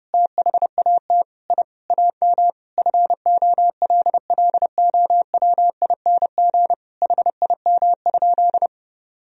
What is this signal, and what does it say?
Morse code, heard as a keyed tone, about 30 wpm, 700 Hz. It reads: THAT I AM FOLLOWING HIM?
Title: Sentences from Top 1000 Words - Review Part 4 - 30wpm